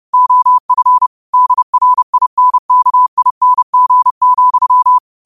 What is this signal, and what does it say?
Morse code, heard as a keyed tone, about 30 wpm, 1000 Hz. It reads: OF DRINKING,
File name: qotdfiles / 30wpm